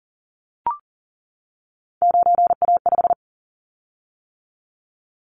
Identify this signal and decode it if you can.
Morse code, heard as a keyed tone, about 42 wpm, 700 Hz. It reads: 9A5
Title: Callsign Prefixes for Top 50 DXCC - 40wpm